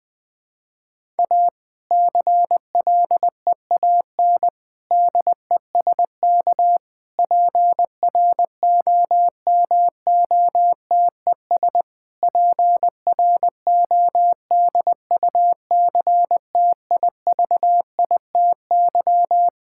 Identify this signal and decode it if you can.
Morse code, heard as a keyed tone, about 20 wpm, 700 Hz. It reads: A CLEAN DESK PROMOTES PRODUCTIVITY
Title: Difficult Sentences with Hints - 20wpm